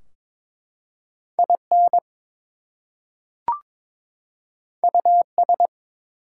Morse code, heard as a keyed tone, about 23 wpm, 700 Hz. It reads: IN US
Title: Two Letter Words - 22wpm